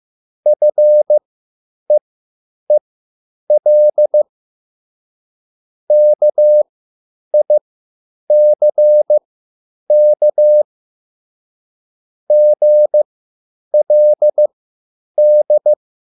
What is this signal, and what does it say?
Morse code, heard as a keyed tone, about 15 wpm, spaced as for 9 wpm, 600 Hz. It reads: FEEL KICK GLD
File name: QSL / CW lesson